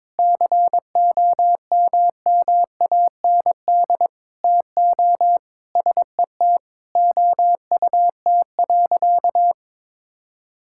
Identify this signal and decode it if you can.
Morse code, heard as a keyed tone, about 22 wpm, 700 Hz. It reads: COMMAND TO SET OUT.
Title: Sentences from Top 1600 Words 22wpm